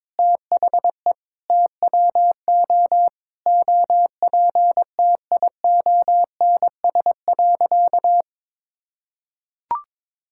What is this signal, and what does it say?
Morse code, heard as a keyed tone, about 22 wpm, 700 Hz. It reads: THE TWO OPTIONS.
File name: practice / 22wpm / Sentences with 3-Word Phrases - 22wpm